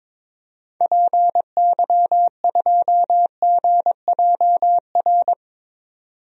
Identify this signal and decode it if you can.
Morse code, heard as a keyed tone, about 22 wpm, 700 Hz. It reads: PY2GJR
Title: DXCC-100 Call Signs - 6 Characters - 22wpm